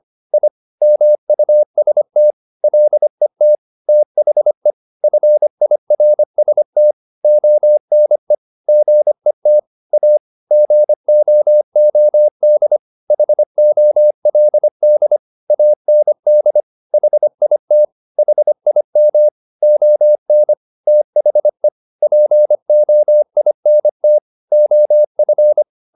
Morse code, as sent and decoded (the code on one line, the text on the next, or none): .. / -- ..- ... - / .-.. - / - .... / ..-. .. .-. ... - / --- -. / --. - / .- / --. --- --- -.. / .... --- .-.. -.. / .- -. -.. / .... .. - / .... .. -- / --- -. / - .... / .--. --- .. -. - / --- ..-.
I MUST LET THE FIRST ONE GET A GOOD HOLD AND HIT HIM ON THE POINT OF